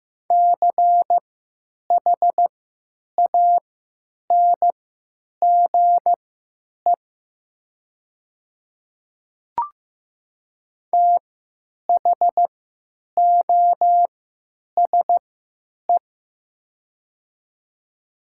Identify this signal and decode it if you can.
Morse code, heard as a keyed tone, about 15 wpm, spaced as for 9 wpm, 700 Hz. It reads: CHANGE THOSE
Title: 3x Character Spacing - Top 300 Words - 15wpm